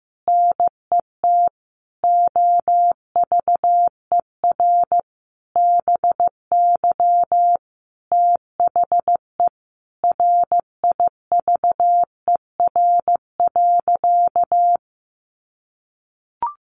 Morse code, heard as a keyed tone, about 15 wpm, 700 Hz. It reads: NET OVER BY THE RIVER.
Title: Sentences from Top 1400 Words 15wpm